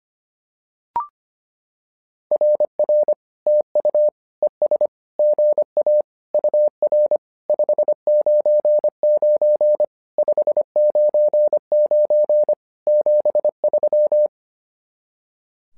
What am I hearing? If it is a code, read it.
Morse code, heard as a keyed tone, about 25 wpm, 600 Hz. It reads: RR TU ES GA UR 599 599 73
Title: SOTA QSO Practice 25wpm